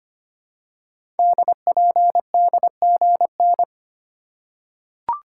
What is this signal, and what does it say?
Morse code, heard as a keyed tone, about 25 wpm, 700 Hz. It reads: DPDGN
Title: Code Groups - 5 Letters - 25wpm